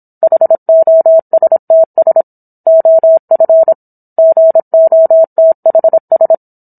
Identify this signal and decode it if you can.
Morse code, heard as a keyed tone, about 27 wpm, 650 Hz. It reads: HOSTS OF GOTHS
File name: Book One / 17-Chapter 17 - 26 WPM